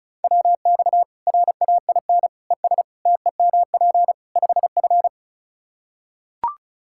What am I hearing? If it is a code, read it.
Morse code, heard as a keyed tone, about 34 wpm, 700 Hz. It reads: WX RAIN ES TEMP 5F
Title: K1USN SST QSO Practice 35wpm